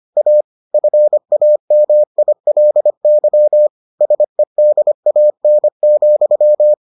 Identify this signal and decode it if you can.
Morse code, heard as a keyed tone, about 25 wpm, 600 Hz. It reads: A FAMILY SEDAN,